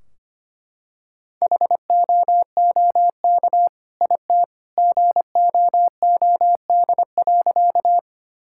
Morse code, heard as a keyed tone, about 25 wpm, 700 Hz. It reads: HOOK IT GOOD.
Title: Sentences from Top 1600 Words 25wpm